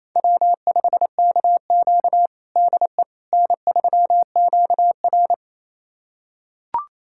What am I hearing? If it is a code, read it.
Morse code, heard as a keyed tone, about 28 wpm, 700 Hz. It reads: W5KQ DE N3QR
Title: USA Callsign DE Callsign - 4 Characters - 28wpm